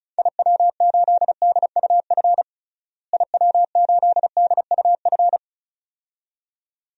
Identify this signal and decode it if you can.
Morse code, heard as a keyed tone, about 36 wpm, 700 Hz. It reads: IW8DUF IW8DUF
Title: DXCC-100 Call Signs - 6 Characters - Repeated 2x - 35wpm